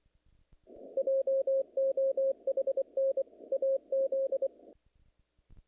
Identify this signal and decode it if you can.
Morse code, heard as a keyed tone, about 25 wpm, 470 Hz. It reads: JOHN AZ
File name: contests / naqp-cw-k5tr-op-k5pi- -2200